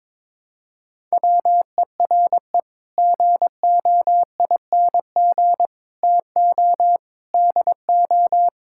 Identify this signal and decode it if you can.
Morse code, heard as a keyed tone, about 22 wpm, 700 Hz. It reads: WERE GOING TO DO
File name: practice / 22wpm / Sets of 8 Words - 22wpm